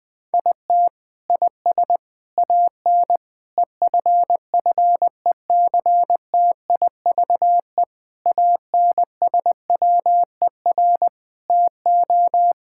Morse code, heard as a keyed tone, about 20 wpm, 700 Hz. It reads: IT IS AN EFFECTIVE ANSWER TO
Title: Sentences from Top 1000 Words - Review Part 6 - 20wpm